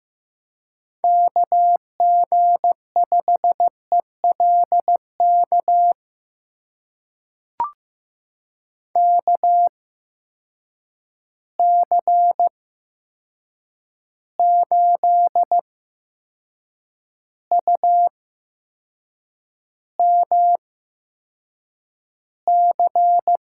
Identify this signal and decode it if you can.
Morse code, heard as a keyed tone, about 15 wpm, spaced as for 6 wpm, 700 Hz. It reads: KG5ELK KC8UMC
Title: USA Call Signs - 6 Characters - 8x Character Spacing - 15wpm